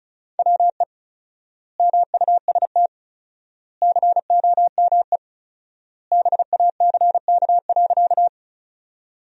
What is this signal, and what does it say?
Morse code, heard as a keyed tone, about 34 wpm, 700 Hz. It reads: WE MUST COME BACK.
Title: 4x Word-Spacing - Sentences from Top 200 Words - 35wpm